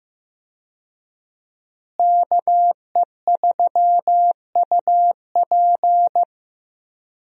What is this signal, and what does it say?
Morse code, heard as a keyed tone, about 15 wpm, 700 Hz. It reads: KE3UP
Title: USA Call Signs - 5 Characters - Repeated 3x - 15wpm